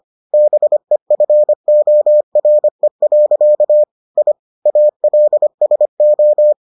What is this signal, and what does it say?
Morse code, heard as a keyed tone, about 25 wpm, 600 Hz. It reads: BEFORE. I ALSO